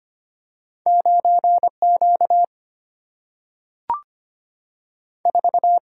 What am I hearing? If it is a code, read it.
Morse code, heard as a keyed tone, about 25 wpm, 700 Hz. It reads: 9Q 4
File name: Two Letter-Number Combinations - 25wpm